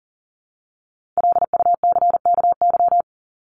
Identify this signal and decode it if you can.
Morse code, heard as a keyed tone, about 40 wpm, 700 Hz. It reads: LUCKY
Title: Top 1751-2000 Words 40wpm